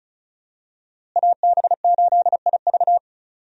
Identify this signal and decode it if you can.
Morse code, heard as a keyed tone, about 36 wpm, 700 Hz. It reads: AB8IV
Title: USA Call Signs - 5 Characters - Repeated 3x - 35wpm